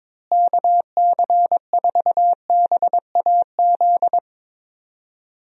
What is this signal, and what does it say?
Morse code, heard as a keyed tone, about 23 wpm, 700 Hz. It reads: KC4BAZ
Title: USA Call Signs - 6 Characters - Repeated 3x - 22wpm